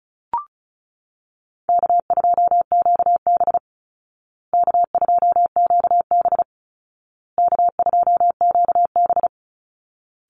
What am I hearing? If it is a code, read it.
Morse code, heard as a keyed tone, about 34 wpm, 700 Hz. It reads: K2QB K2QB K2QB
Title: Easy Call Signs Repeated 3x - 35wpm